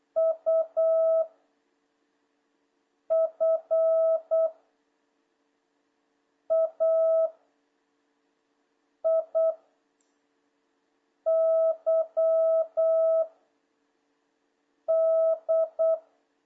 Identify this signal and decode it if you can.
Morse code, heard as a keyed tone, about 8 wpm, 660 Hz. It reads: U F A I Y D
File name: jh8cbh / 2026cw